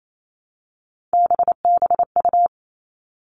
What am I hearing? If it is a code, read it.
Morse code, heard as a keyed tone, about 28 wpm, 700 Hz. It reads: BBU
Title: Three Challenging Letter-Number Combinations 28wpm